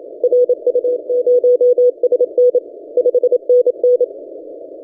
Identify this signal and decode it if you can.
Morse code, heard as a keyed tone, about 27 wpm, 490 Hz. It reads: RU0SN 5NN
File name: ft5zm-17cw-2014-01-27-0120z-lp